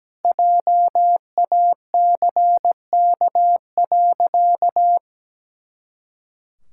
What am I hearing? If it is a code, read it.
Morse code, heard as a keyed tone, about 17 wpm, 700 Hz. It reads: JACK.
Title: Pangram Challenge v2 - 17wpm